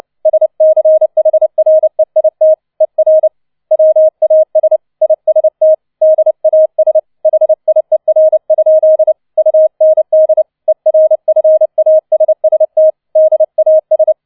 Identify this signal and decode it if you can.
Morse code, heard as a keyed tone, about 29 wpm, 630 Hz. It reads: SCHREIT ER WAS IST DAS HIER? UND ERFASST DAS